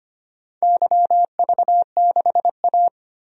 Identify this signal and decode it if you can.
Morse code, heard as a keyed tone, about 25 wpm, 700 Hz. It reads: YV6A
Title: DXCC-100 Call Signs - 4 Characters - 25wpm